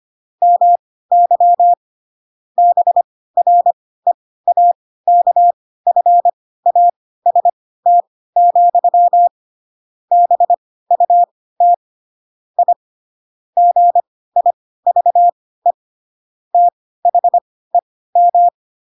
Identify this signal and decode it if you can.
Morse code, heard as a keyed tone, about 25 wpm, spaced as for 16 wpm, 700 Hz. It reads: MY BREAKFAST, BUT I GIVE THEM
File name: Pinocchio / Pinocchio ch7 10 25 ik1hge